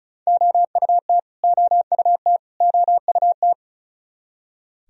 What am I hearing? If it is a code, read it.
Morse code, heard as a keyed tone, about 36 wpm, 700 Hz. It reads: OUT OUT OUT
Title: Top 100 Words Repeated 35wpm